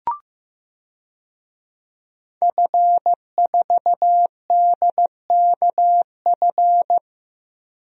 Morse code, heard as a keyed tone, about 15 wpm, 700 Hz. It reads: F4DKF
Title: DXCC-100 Call Signs - 5 Characters - Repeated 2x - 15wpm